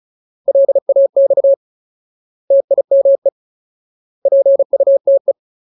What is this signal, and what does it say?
Morse code, heard as a keyed tone, about 35 wpm, 550 Hz. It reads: LAX TIME PUTE
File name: Ultimate Word List / 35WPM Ultimate